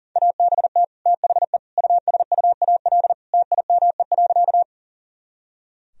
Morse code, heard as a keyed tone, about 40 wpm, 700 Hz. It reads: ABT THE USUAL TIME.
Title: Sentences from Top 400 Words and Abbreviations 40wpm